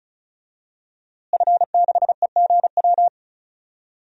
Morse code, heard as a keyed tone, about 34 wpm, 700 Hz. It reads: F6EGW
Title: DXCC-100 Call Signs - 5 Characters - Repeated 3x - 35wpm